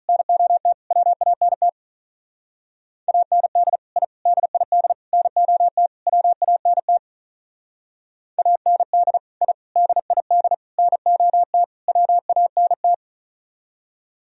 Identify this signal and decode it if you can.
Morse code, heard as a keyed tone, about 44 wpm, 700 Hz. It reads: NOT WANT AND I DID NOT WANT AND I DID NOT WANT